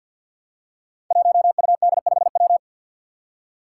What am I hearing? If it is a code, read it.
Morse code, heard as a keyed tone, about 52 wpm, 700 Hz. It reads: 1UD5W
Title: Five Challenging Letter-Number Combinations 50wpm